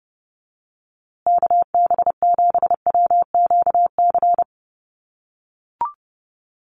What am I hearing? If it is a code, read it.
Morse code, heard as a keyed tone, about 30 wpm, 700 Hz. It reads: KB7WQC
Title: USA Call Signs Repeated 2x 30wpm